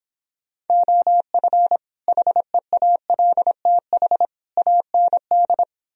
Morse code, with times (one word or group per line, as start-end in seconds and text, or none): 0.7-1.8 OF
2.1-4.3 HEALTH
4.6-5.6 AND